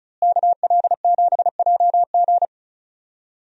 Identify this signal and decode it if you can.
Morse code, heard as a keyed tone, about 34 wpm, 700 Hz. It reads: KL7JG
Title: DXCC-100 Call Signs - 5 Characters - Repeated 2x - 35wpm